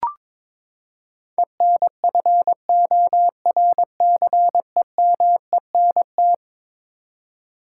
Morse code, heard as a keyed tone, about 22 wpm, 700 Hz. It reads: ENFORCEMENT
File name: practice / 22wpm / Eleven Letter Words - 22wpm